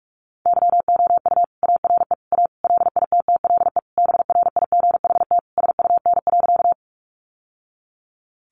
Morse code, heard as a keyed tone, about 44 wpm, 700 Hz. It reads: YOU ARE A LITTLE BRIGHT SUN.
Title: Sentences from Top 1000 Words 45wpm